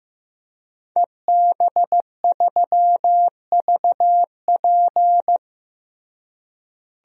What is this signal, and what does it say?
Morse code, heard as a keyed tone, about 16 wpm, 700 Hz. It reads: EB3VP
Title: DXCC-100 Call Signs - 5 Characters - Repeated 3x - 15wpm